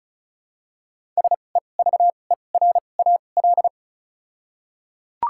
Morse code, heard as a keyed tone, about 36 wpm, spaced as for 25 wpm, 700 Hz. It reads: SEVERAL
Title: 2x Character Spacing - Top 500 Words - 35wpm